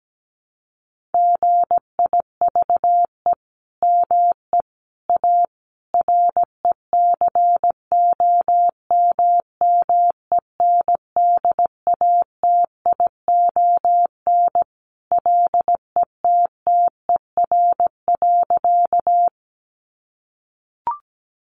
Morse code, heard as a keyed tone, about 17 wpm, 700 Hz. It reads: GIVE ME A RECOMMENDATION LETTER.
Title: Sentences from Top 1400 Words 17wpm